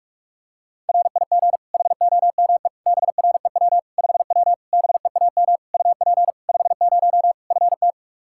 Morse code, heard as a keyed tone, about 46 wpm, 700 Hz. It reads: RIG HOME BREW 5W BEAM UP 50 FT
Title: Practice QSO - Exchange 2 - Info - 45wpm